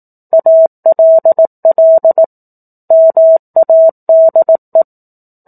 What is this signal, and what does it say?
Morse code, heard as a keyed tone, about 18 wpm, 650 Hz. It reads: ALL MADE
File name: 04-Chapter-04 18 WPM